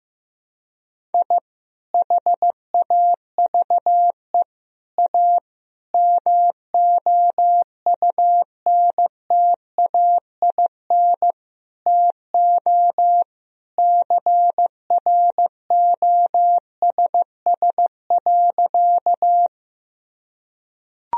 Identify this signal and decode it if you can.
Morse code, heard as a keyed tone, about 16 wpm, 700 Hz. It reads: I HAVE A MOUNTAIN TO CROSS.
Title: Sentences from Top 1000 Words - Review Part 1 - 15wpm